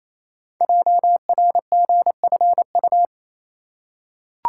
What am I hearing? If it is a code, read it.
Morse code, heard as a keyed tone, about 28 wpm, 700 Hz. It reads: JRGFU
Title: Code Groups - 5 Letters - 28wpm